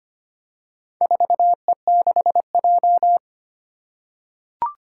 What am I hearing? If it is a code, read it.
Morse code, heard as a keyed tone, about 25 wpm, 700 Hz. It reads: 4E6J
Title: Four Challenging Letter-Number Combinations 25wpm